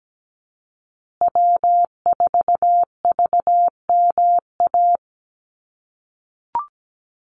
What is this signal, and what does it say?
Morse code, heard as a keyed tone, about 17 wpm, 700 Hz. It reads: W4VMA
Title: USA Call Signs Repeated 2x 17wpm